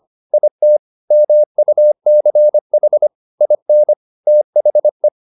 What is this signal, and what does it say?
Morse code, heard as a keyed tone, about 25 wpm, 600 Hz. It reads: IT MUCH IN THE
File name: the old man and the sea part 1 25wpm